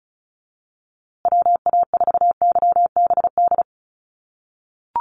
Morse code, heard as a keyed tone, about 34 wpm, 700 Hz. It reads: WA4YBD
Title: USA Call Signs Repeated 3x 35wpm